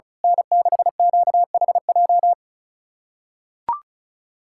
Morse code, heard as a keyed tone, about 34 wpm, 700 Hz. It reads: N6QHJ
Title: USA Callsign DE Callsign - 5 Characters - 35wpm